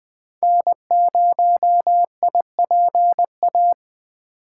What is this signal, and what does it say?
Morse code, heard as a keyed tone, about 20 wpm, 700 Hz. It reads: N0IPA
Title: 2023 SOTA Chaser Call Signs - 20wpm